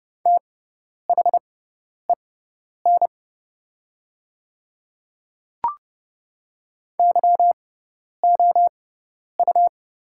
Morse code, heard as a keyed tone, about 30 wpm, spaced as for 10 wpm, 700 Hz. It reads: THEN YOU
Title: 6x Character Spacing - Top 100 Words - 30wpm